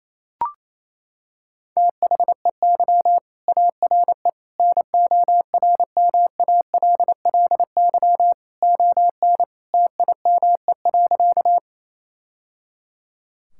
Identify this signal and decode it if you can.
Morse code, heard as a keyed tone, about 28 wpm, 700 Hz. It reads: THEY ARE NORMALLY ON TIME.
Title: Sentences from Top 1000 Words - Review Part 6 - 28wpm